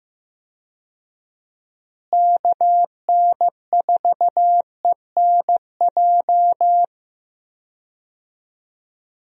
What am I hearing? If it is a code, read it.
Morse code, heard as a keyed tone, about 15 wpm, 700 Hz. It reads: KN4ENJ
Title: USA Call Signs - 6 Characters - Repeated 2x - 15wpm